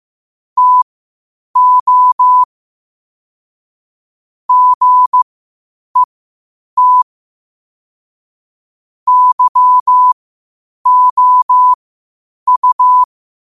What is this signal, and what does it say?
Morse code, heard as a keyed tone, about 15 wpm, spaced as for 8 wpm, 1000 Hz. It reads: TO GET YOU
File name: qotdfiles / 8wpm